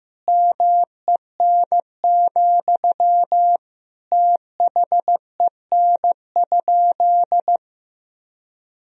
Morse code, heard as a keyed tone, about 15 wpm, 700 Hz. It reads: MEN, THEN?
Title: Sentences from Top 1000 Words - Review Part 1 - 15wpm